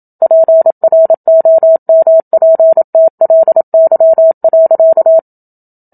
Morse code, heard as a keyed tone, about 27 wpm, 650 Hz. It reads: PROMPTLY.
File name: Book Two / 01-Chapter 1 - 27 WPM